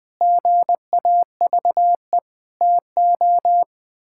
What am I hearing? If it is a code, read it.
Morse code, heard as a keyed tone, about 20 wpm, 700 Hz. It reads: GAVE TO
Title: Sentences with 8-Word Phrases - 20wpm